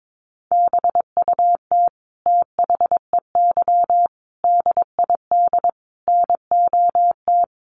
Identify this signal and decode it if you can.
Morse code, heard as a keyed tone, about 22 wpm, 700 Hz. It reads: BUT THEY DID NOT
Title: Sentences from Top 1300 Words 22wpm